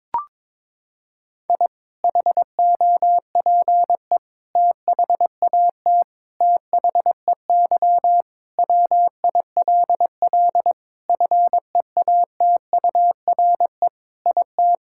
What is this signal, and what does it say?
Morse code, heard as a keyed tone, about 21 wpm, 700 Hz. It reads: I HOPE THAT THEY WILL FEATURE IT
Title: Sentences from Top 1000 Words - Review Part 5 - 22wpm